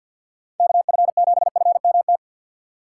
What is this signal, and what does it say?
Morse code, heard as a keyed tone, about 53 wpm, 700 Hz. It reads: KF7FMT
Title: USA Call Signs - 6 Characters - 3x Character Spacing - 50wpm